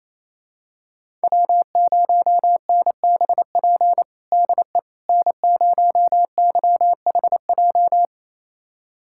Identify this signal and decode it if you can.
Morse code, heard as a keyed tone, about 28 wpm, 700 Hz. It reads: W0NBP DE N0YHJ
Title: USA Callsign DE Callsign - 5 Characters - 28wpm